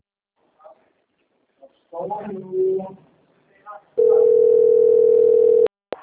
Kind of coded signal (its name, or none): telephone call progress tone